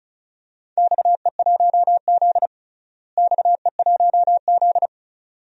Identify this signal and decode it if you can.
Morse code, heard as a keyed tone, about 36 wpm, 700 Hz. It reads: XE1Z XE1Z
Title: DXCC-100 Call Signs - 4 Characters - Repeated 2x - 35wpm